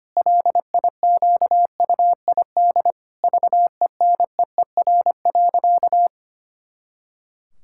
Morse code, heard as a keyed tone, about 25 wpm, 700 Hz. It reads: LIQUID VENEER.
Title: Pangram Challenge v2 - 25wpm